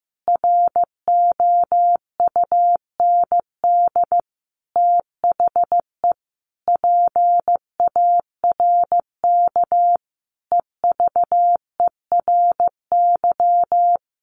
Morse code, heard as a keyed tone, about 15 wpm, 700 Hz. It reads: ROUND THE PARK EVERY
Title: Sentences from Top 1300 Words 15wpm